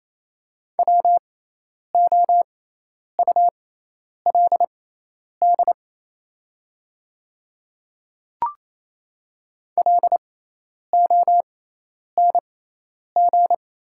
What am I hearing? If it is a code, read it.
Morse code, heard as a keyed tone, about 28 wpm, spaced as for 10 wpm, 700 Hz. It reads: WOULD LONG
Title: 6x Character Spacing - Top 200 Words - 28wpm